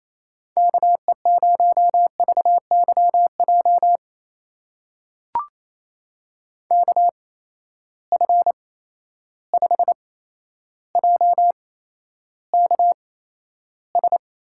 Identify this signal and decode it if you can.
Morse code, heard as a keyed tone, about 28 wpm, spaced as for 12 wpm, 700 Hz. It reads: KE0VYJ KF5JKS
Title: USA Call Signs - 6 Characters - 8x Character Spacing - 28wpm